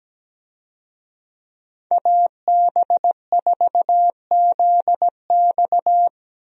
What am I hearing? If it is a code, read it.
Morse code, heard as a keyed tone, about 17 wpm, 700 Hz. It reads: AB4ZX
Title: USA Call Signs - 5 Characters - 17wpm